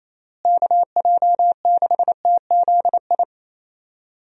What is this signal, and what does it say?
Morse code, heard as a keyed tone, about 27 wpm, 700 Hz. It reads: KJ6TZI
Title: USA Call Signs - 6 Characters - Repeated 2x - 28wpm